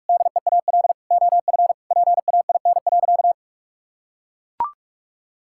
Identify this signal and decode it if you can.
Morse code, heard as a keyed tone, about 41 wpm, 700 Hz. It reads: DEAL OF PAIN.
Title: Sentences from Top 1600 Words 45wpm